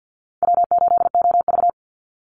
Morse code, heard as a keyed tone, about 50 wpm, 700 Hz. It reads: W8OV
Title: Easy Call Signs Repeated 3x - 50wpm